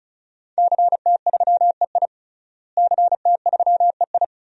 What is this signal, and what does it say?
Morse code, heard as a keyed tone, about 34 wpm, 700 Hz. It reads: CT3EI CT3EI
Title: DXCC-100 Call Signs - 5 Characters - Repeated 2x - 35wpm